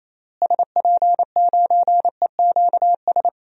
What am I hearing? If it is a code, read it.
Morse code, heard as a keyed tone, about 28 wpm, 700 Hz. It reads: SP9EQS
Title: DXCC-100 Call Signs - 6 Characters - Repeated 3x - 28wpm